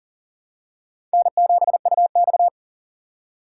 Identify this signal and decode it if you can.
Morse code, heard as a keyed tone, about 40 wpm, 700 Hz. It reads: N7UX
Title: USA Call Signs - 4 Characters - 4x Character Spacing - 40wpm